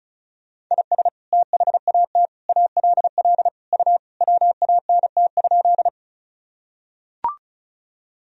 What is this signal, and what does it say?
Morse code, heard as a keyed tone, about 36 wpm, 690 Hz. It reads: IS THAT ALL U WANT?
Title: Sentences from Top 200 Words and Abbreviations 35wpm